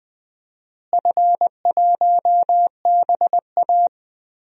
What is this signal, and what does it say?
Morse code, heard as a keyed tone, about 21 wpm, 700 Hz. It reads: F1BA
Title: DXCC-100 Call Signs - 4 Characters - Repeated 2x - 20wpm